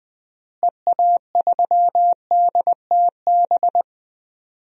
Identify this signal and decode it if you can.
Morse code, heard as a keyed tone, about 20 wpm, 700 Hz. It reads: EA3DTB
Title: DXCC-100 Call Signs - 6 Characters - Repeated 3x - 20wpm